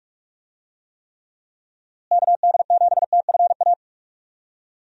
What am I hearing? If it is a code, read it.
Morse code, heard as a keyed tone, about 41 wpm, 700 Hz. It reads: KD7TFA
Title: USA Call Signs - 6 Characters - Repeated 3x - 45wpm